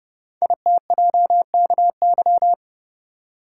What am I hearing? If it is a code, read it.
Morse code, heard as a keyed tone, about 30 wpm, 700 Hz. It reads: ITJKY
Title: Code Groups - 5 Letters - 30wpm